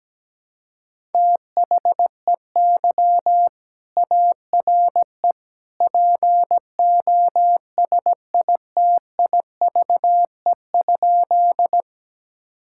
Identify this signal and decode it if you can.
Morse code, heard as a keyed tone, about 17 wpm, 700 Hz. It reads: THEY ARE POSITIVE?
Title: Sentences from Top 1000 Words - Review Part 5 - 17wpm